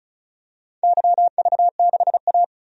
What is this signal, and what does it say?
Morse code, heard as a keyed tone, about 34 wpm, 700 Hz. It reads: YV6A
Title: DXCC-100 Call Signs - 4 Characters - 35wpm